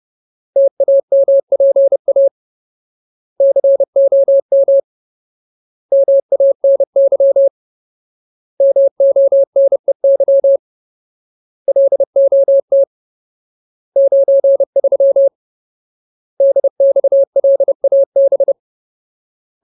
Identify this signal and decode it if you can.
Morse code, heard as a keyed tone, about 30 wpm, 550 Hz. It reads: TAMPA COM MANY MONEY LOT 93 DXLAB
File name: Ultimate Word List / 30WPM Ultimate